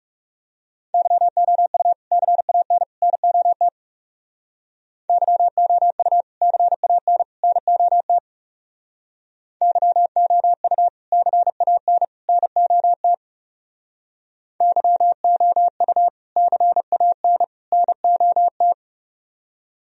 Morse code, heard as a keyed tone, about 36 wpm, 700 Hz. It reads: YOU CAN NOT YOU CAN NOT YOU CAN NOT YOU CAN NOT
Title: Speed-Racing - Sets of 3 Words 30wpm